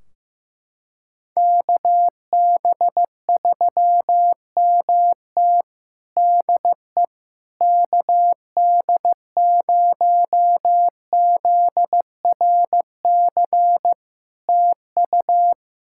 Morse code, heard as a keyed tone, about 15 wpm, 700 Hz. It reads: KB3MT DE KD0ZRC TU
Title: Practice QSO - Exchange 4 - Ending - 15wpm